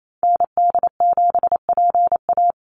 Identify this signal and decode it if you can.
Morse code, heard as a keyed tone, about 28 wpm, 700 Hz. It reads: ND7PA
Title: NA SOTA Activator Callsigns 28wpm